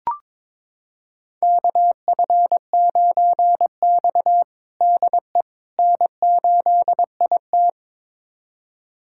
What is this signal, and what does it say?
Morse code, heard as a keyed tone, about 23 wpm, 700 Hz. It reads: KF9X DE N8IT
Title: USA Callsign DE Callsign - 4 Characters - 22wpm